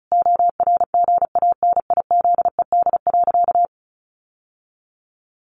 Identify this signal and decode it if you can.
Morse code, heard as a keyed tone, about 34 wpm, 700 Hz. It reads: ORGANIZED.
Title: Sentences from Top 1300 Words 35wpm